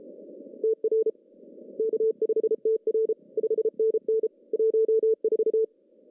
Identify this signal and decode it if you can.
Morse code, heard as a keyed tone, about 32 wpm, 410 Hz. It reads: TR K5TR 5NN 14
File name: CW / cqww-cw-k5tr